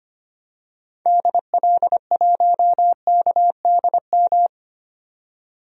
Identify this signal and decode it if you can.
Morse code, heard as a keyed tone, about 25 wpm, 700 Hz. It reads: DL1KDM
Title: DXCC-100 Call Signs - 6 Characters - 25wpm